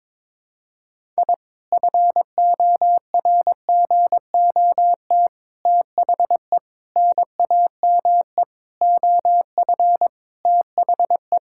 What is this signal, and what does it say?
Morse code, heard as a keyed tone, about 22 wpm, 700 Hz. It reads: I FORGOT THE NAME OF THE
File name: practice / 22wpm / Sentences with 4-Word Phrases - 22wpm